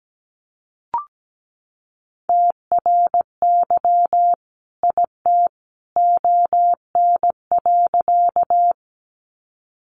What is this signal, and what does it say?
Morse code, heard as a keyed tone, about 17 wpm, 700 Hz. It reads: TRY IT ON.
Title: Sentences from Top 200 Words 17wpm